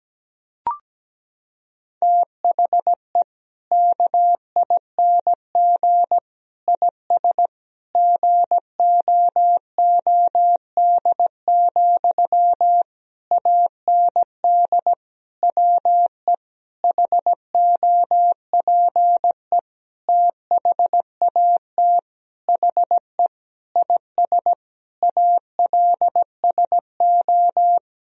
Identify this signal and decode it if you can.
Morse code, heard as a keyed tone, about 17 wpm, 700 Hz. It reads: THE KING IS GOOD, AND WE HOPE THAT HE IS ALSO
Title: Sentences from Top 1000 Words - Review Part 1 - 17wpm